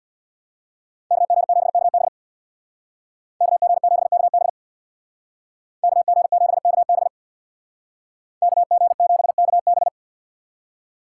Morse code, heard as a keyed tone, about 64 wpm, 690 Hz. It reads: KG7KB KG7KB KG7KB KG7KB